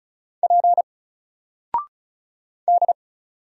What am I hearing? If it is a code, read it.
Morse code, heard as a keyed tone, about 34 wpm, 700 Hz. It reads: P D
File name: Single-Letter-Number-35wpm